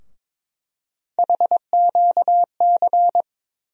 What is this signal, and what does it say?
Morse code, heard as a keyed tone, about 21 wpm, 700 Hz. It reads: HQC
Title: USA Call Sign Postfixes - 3 Letters 22wpm